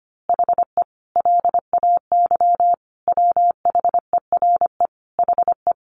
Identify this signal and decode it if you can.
Morse code, heard as a keyed tone, about 25 wpm, 700 Hz. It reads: HE LAY WHERE HE
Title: A Sound Of Thunder 25wpm